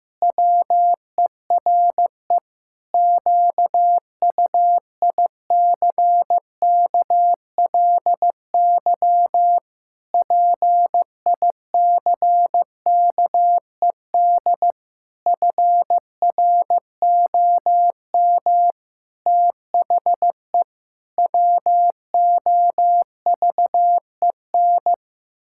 Morse code, heard as a keyed tone, about 15 wpm, 700 Hz. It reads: WERE QUICKLY PICKED FROM THE WOVEN
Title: Pangram Challenge v2 - 15wpm